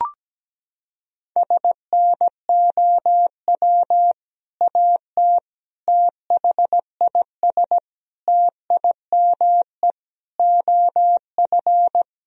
Morse code, heard as a keyed tone, about 17 wpm, 700 Hz. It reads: SNOW AT THIS TIME OF
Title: Sentences from Top 1000 Words - Review Part 3 - 17wpm